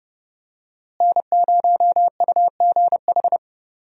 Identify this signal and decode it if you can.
Morse code, heard as a keyed tone, about 30 wpm, 700 Hz. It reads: N0UGH
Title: USA Call Signs - 5 Characters - Repeated 3x - 30wpm